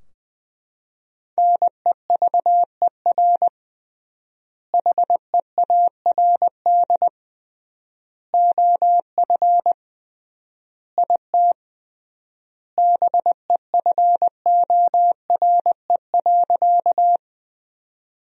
Morse code, heard as a keyed tone, about 20 wpm, 700 Hz. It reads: NEVER HEARD OF IT BEFORE.
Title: 3x Word-Spacing - Sentences from Top 400 Words - 20wpm